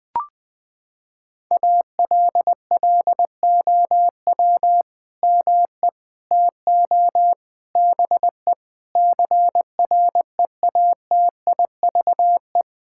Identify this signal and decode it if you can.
Morse code, heard as a keyed tone, about 20 wpm, 700 Hz. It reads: ALLOW ME TO BE CREATIVE
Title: Sentences from Top 1000 Words - Review Part 5 - 20wpm